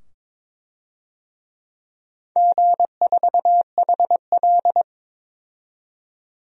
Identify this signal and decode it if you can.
Morse code, heard as a keyed tone, about 21 wpm, 700 Hz. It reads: G4HL
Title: DXCC-100 Call Signs - 4 Characters - 22wpm